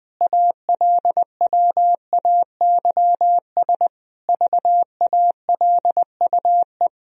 Morse code, heard as a keyed tone, about 21 wpm, 700 Hz. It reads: ALWAYS VALUE
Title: Sentences from Top 1000 Words - Review Part 3 - 20wpm